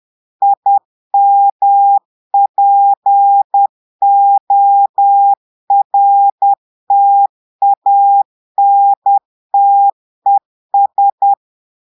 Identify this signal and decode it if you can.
Morse code, heard as a keyed tone, about 10 wpm, 800 Hz. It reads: IMPORTANTES
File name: portugais texte en clair 2 vitesse 10 mots minutes